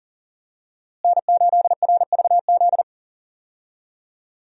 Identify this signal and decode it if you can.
Morse code, heard as a keyed tone, about 37 wpm, 700 Hz. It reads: N8RVZ
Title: USA Call Signs - 5 Characters - Repeated 2x - 40wpm